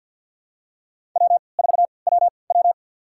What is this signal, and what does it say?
Morse code, heard as a keyed tone, about 48 wpm, spaced as for 28 wpm, 700 Hz. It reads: W4WW
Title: USA Call Signs - 3x Character Spacing - 50wpm